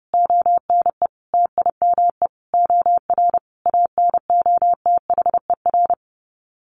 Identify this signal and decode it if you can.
Morse code, heard as a keyed tone, about 30 wpm, 700 Hz. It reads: ONE TIME OR ANOTHER
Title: Sets of 5 Words Encore 30wpm